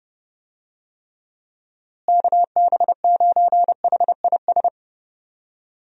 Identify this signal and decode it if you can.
Morse code, heard as a keyed tone, about 32 wpm, 700 Hz. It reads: KB9HIS
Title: USA Call Signs - 6 Characters - Repeated 2x - 30wpm